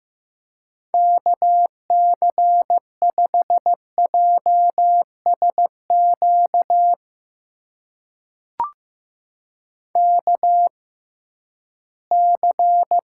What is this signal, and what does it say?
Morse code, heard as a keyed tone, about 15 wpm, spaced as for 11 wpm, 700 Hz. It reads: KC5JSQ KC
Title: USA Call Signs - 6x Character Spacing - 15wpm